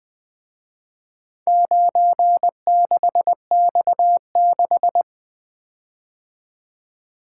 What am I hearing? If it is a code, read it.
Morse code, heard as a keyed tone, about 22 wpm, 700 Hz. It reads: 96X6
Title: Speed-Racing - Four Letter-Number Combinations - 20wpm